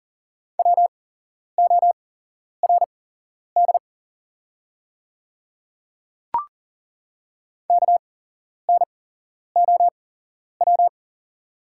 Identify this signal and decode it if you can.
Morse code, heard as a keyed tone, about 40 wpm, spaced as for 11 wpm, 700 Hz. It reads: WORD KNOW
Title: 8x Character Spacing - Top 100 Words - 40wpm